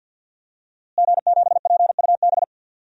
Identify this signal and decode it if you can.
Morse code, heard as a keyed tone, about 53 wpm, 700 Hz. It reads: G7PUB
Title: DXCC-100 Call Signs - 5 Characters - Repeated 3x - 50wpm